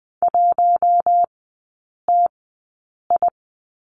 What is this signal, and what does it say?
Morse code, heard as a keyed tone, about 20 wpm, spaced as for 8 wpm, 700 Hz. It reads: 1TI
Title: 047 Lesson 9e - Number 4 - Three Characters 20wpm